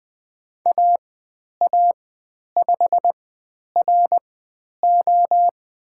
Morse code, heard as a keyed tone, about 20 wpm, spaced as for 10 wpm, 700 Hz. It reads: AA5RO
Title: 109 Lesson 18g - Letter m - Call Signs 20wpm